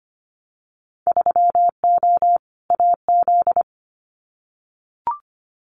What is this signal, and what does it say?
Morse code, heard as a keyed tone, about 25 wpm, 700 Hz. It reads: 3O AZ
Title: Winter Field Day Exchanges 25wpm